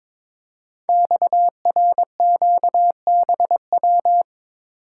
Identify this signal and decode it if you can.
Morse code, heard as a keyed tone, about 23 wpm, 700 Hz. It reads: XRQBW
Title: Code Groups - 5 Letters - 22wpm